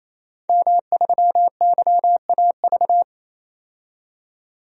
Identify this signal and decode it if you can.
Morse code, heard as a keyed tone, about 28 wpm, 700 Hz. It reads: M3YAV
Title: DXCC-100 Call Signs - 5 Characters - 28wpm